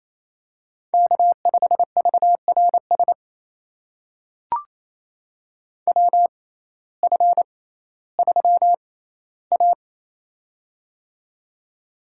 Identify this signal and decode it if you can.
Morse code, heard as a keyed tone, about 28 wpm, spaced as for 15 wpm, 700 Hz. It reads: K5VRS WF3A
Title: USA Call Signs - 6x Character Spacing - 28wpm